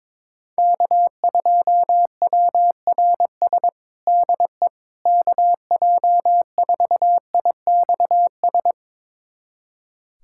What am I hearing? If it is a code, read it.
Morse code, heard as a keyed tone, about 22 wpm, 700 Hz. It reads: K2WRS DE KJ4IXS
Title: USA Callsign DE Callsign 22wpm